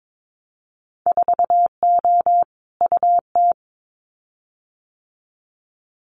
Morse code, heard as a keyed tone, about 22 wpm, 700 Hz. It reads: 4O UT
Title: Winter Field Day Exchanges Encore 22wpm